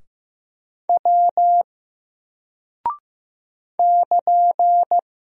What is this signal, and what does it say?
Morse code, heard as a keyed tone, about 15 wpm, 700 Hz. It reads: W <KN>
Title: Single Character (Letter:Number:Punctuation:Prosign) 15wpm